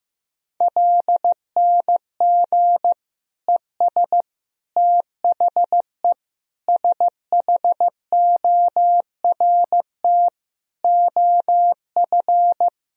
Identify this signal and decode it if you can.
Morse code, heard as a keyed tone, about 15 wpm, 700 Hz. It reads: LNG ES THE SHORT OF
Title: Sentences from Top 400 Words and Abbreviations 15wpm